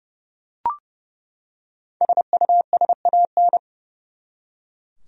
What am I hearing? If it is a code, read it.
Morse code, heard as a keyed tone, about 30 wpm, 700 Hz. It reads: SUSAN
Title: USA Common Names - Max 5 Letters - 30wpm